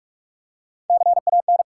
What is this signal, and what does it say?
Morse code, heard as a keyed tone, about 44 wpm, 700 Hz. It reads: CAN
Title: Top 180 QSO Elements 45wpm